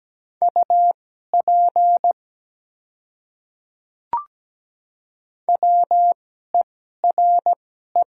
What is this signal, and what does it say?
Morse code, heard as a keyed tone, about 17 wpm, spaced as for 12 wpm, 700 Hz. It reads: UP WERE
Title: 2x Character Spacing - Top 200 Words - 17wpm